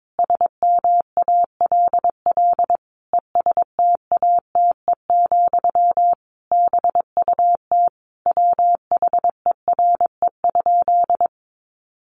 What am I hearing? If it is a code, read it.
Morse code, heard as a keyed tone, about 22 wpm, 700 Hz. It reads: SMALL ESTATE, BUT WHERE?
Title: Sentences from Top 1200 Words 22wpm